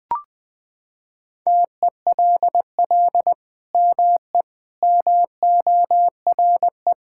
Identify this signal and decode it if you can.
Morse code, heard as a keyed tone, about 21 wpm, 700 Hz. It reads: TELL ME MORE
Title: Sentences from Top 1000 Words - Review Part 5 - 20wpm